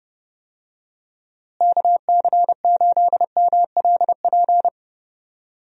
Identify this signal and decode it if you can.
Morse code, heard as a keyed tone, about 32 wpm, 700 Hz. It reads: KC8MLP
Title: USA Call Signs - 6 Characters - Repeated 3x - 30wpm